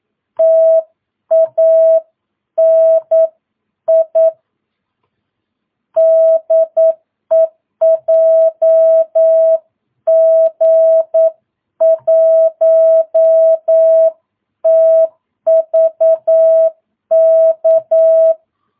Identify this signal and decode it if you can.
Morse code, heard as a keyed tone, about 9 wpm, 650 Hz. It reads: TANI DE JG1TVK
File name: jh8cbh / 2022cw